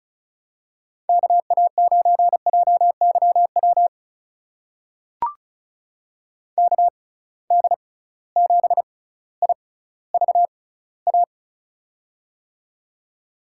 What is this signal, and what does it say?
Morse code, heard as a keyed tone, about 36 wpm, spaced as for 18 wpm, 700 Hz. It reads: KA9JYW KD7IVA
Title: USA Call Signs - 6x Character Spacing - 35wpm